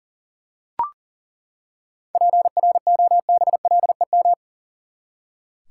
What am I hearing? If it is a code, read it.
Morse code, heard as a keyed tone, about 39 wpm, 700 Hz. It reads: PROBLEM
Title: Seven Letter Words - 40wpm